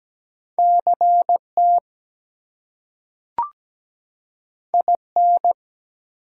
Morse code, heard as a keyed tone, about 17 wpm, 700 Hz. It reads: CT IN
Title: US States and Canadian Provinces 17wpm